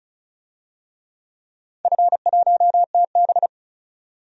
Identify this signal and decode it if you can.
Morse code, heard as a keyed tone, about 34 wpm, 700 Hz. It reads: F1TB
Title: DXCC-100 Call Signs - 4 Characters - 35wpm